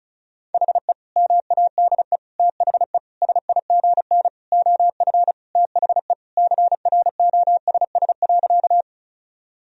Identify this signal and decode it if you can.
Morse code, heard as a keyed tone, about 36 wpm, 700 Hz. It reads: HE MADE THE SIGN OF THE CROSS.
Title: Sentences from Top 1000 Words - Review Part 1 - 35wpm